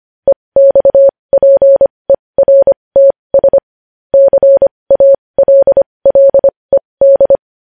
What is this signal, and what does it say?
Morse code, heard as a keyed tone, about 25 wpm, spaced as for 20 wpm, 550 Hz. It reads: EXPERTS CALLED
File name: podcast / News headlines-1x-25x20